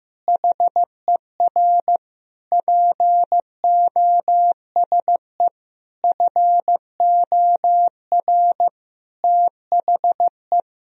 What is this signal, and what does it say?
Morse code, heard as a keyed tone, about 15 wpm, 700 Hz. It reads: HER POSE FOR THE
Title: Sentences from Top 1000 Words - Review Part 2 - 15wpm